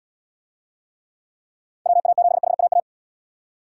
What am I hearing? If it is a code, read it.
Morse code, heard as a keyed tone, about 79 wpm, 700 Hz. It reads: WA7HAA